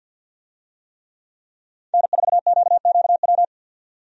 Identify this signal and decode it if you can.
Morse code, heard as a keyed tone, about 52 wpm, 700 Hz. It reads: N4QQW
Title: USA Call Signs - 5 Characters - 50wpm